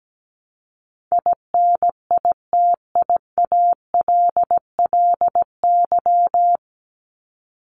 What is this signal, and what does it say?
Morse code, heard as a keyed tone, about 17 wpm, 700 Hz. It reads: INITIALLY
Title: Top 1251-1500 Words 17wpm